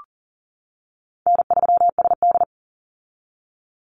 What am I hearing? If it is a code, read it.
Morse code, heard as a keyed tone, about 40 wpm, 700 Hz. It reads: N3SD
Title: Easy Call Signs - 40wpm